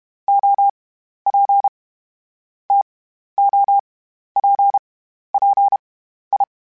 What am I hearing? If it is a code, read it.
Morse code, heard as a keyed tone, about 32 wpm, 800 Hz. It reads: O P T O P P I
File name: Misc / Misc-mp3s / TO @30